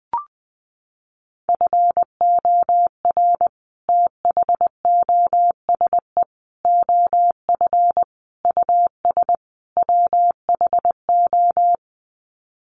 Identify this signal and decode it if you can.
Morse code, heard as a keyed tone, about 20 wpm, 700 Hz. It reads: FOR THOSE OF US WHO
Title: Sets of 5 Words Encore 20wpm